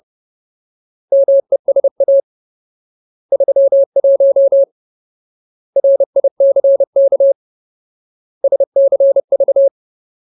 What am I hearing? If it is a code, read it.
Morse code, heard as a keyed tone, about 30 wpm, 550 Hz. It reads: MESA 31 RICK SCV
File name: Ultimate Word List / 30WPM Ultimate